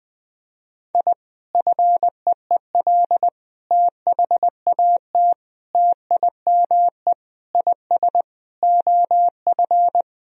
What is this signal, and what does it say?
Morse code, heard as a keyed tone, about 20 wpm, 700 Hz. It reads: I FEEL THAT TIME IS OF
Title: Sentences from Top 1000 Words - Review Part 3 - 20wpm